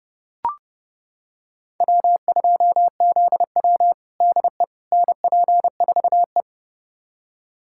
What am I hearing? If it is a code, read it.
Morse code, heard as a keyed tone, about 32 wpm, 700 Hz. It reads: W2ZW DE NP4E
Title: USA Callsign DE Callsign - 4 Characters - 30wpm